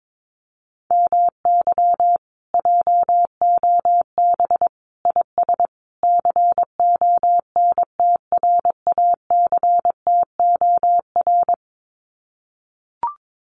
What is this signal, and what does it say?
Morse code, heard as a keyed tone, about 22 wpm, 700 Hz. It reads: MY JOB IS CONTRACTOR
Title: Common QSO Phrases Encore 22wpm